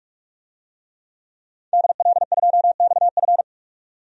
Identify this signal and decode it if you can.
Morse code, heard as a keyed tone, about 41 wpm, 700 Hz. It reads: DL2XF
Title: DXCC-100 Call Signs - 5 Characters - 45wpm